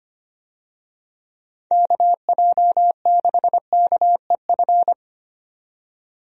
Morse code, heard as a keyed tone, about 25 wpm, 700 Hz. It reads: KJ6KEF
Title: USA Call Signs - 6 Characters - 25wpm